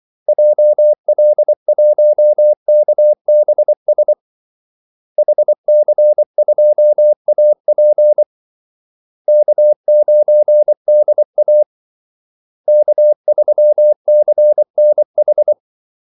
Morse code, heard as a keyed tone, about 24 wpm, 600 Hz. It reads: JL1KBS HC2AP K9DA K3CNH
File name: Callsigns-24wpm